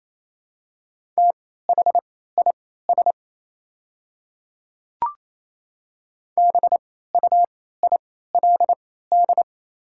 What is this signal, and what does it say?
Morse code, heard as a keyed tone, about 28 wpm, spaced as for 16 wpm, 700 Hz. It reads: THIS BUILD
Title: 3x Character Spacing - Top 200 Words - 28wpm